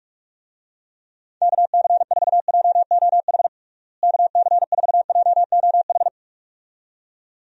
Morse code, heard as a keyed tone, about 46 wpm, 700 Hz. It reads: KC4JOH KC4JOH
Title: USA Call Signs - 6 Characters - Repeated 2x - 45wpm